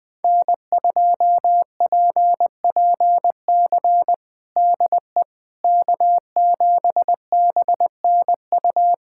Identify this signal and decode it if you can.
Morse code, heard as a keyed tone, about 20 wpm, 700 Hz. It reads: N2PPC DE K7BNU
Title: USA Callsign DE Callsign - 5 Characters - 20wpm